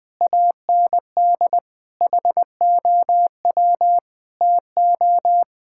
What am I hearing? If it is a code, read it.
Morse code, heard as a keyed tone, about 21 wpm, 700 Hz. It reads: AND HOW TO
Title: Sets of 8 Words - 20wpm